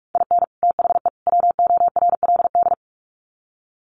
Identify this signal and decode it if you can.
Morse code, heard as a keyed tone, about 44 wpm, 700 Hz. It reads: IN THE WORLD
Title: Sets of 3 Words 45wpm